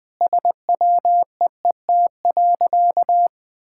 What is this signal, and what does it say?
Morse code, heard as a keyed tone, about 20 wpm, 700 Hz. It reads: SWEET.
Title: Sentences with 3-Word Phrases - 20wpm